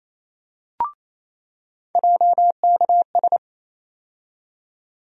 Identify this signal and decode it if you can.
Morse code, heard as a keyed tone, about 28 wpm, 700 Hz. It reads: JKS
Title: USA Call Sign Postfixes - 3 Letters 28wpm